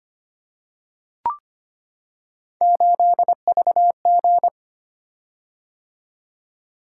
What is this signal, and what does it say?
Morse code, heard as a keyed tone, about 25 wpm, 700 Hz. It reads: 8VG
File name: Three Letter-Number Combinations - 25wpm